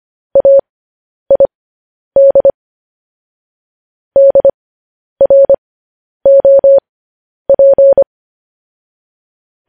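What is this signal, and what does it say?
Morse code, heard as a keyed tone, about 25 wpm, spaced as for 10 wpm, 550 Hz. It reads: AID DROP
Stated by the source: Morse code news headlines practice at 10 wpm.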